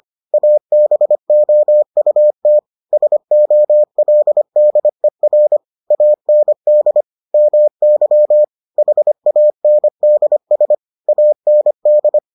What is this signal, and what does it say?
Morse code, heard as a keyed tone, about 25 wpm, 600 Hz. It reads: ABOUT SOLDER AND MY HANDS AND